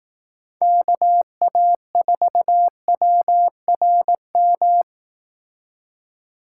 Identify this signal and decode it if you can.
Morse code, heard as a keyed tone, about 18 wpm, 700 Hz. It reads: KA4WRM
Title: Speed Racing - USA 2x3 Call Signs 15wpm